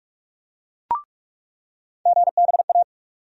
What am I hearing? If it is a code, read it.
Morse code, heard as a keyed tone, about 47 wpm, 700 Hz. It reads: GBA
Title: USA Call Sign Postfixes - 3 Letters 45wpm